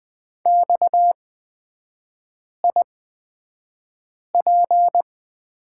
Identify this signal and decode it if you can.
Morse code, heard as a keyed tone, about 20 wpm, spaced as for 5 wpm, 700 Hz. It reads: XIP
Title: 250 Lesson 40e - BK - Three Characters 20wpm